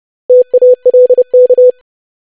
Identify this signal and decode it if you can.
Morse code, heard as a keyed tone, about 30 wpm, 500 Hz. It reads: TALK
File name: lesson / web / basic500-25wpm